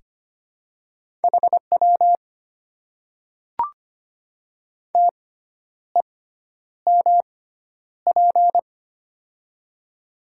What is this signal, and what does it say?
Morse code, heard as a keyed tone, about 25 wpm, spaced as for 10 wpm, 700 Hz. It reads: HW TEMP
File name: Top 90 QSO Elements - 6x Character Spacing - 25wpm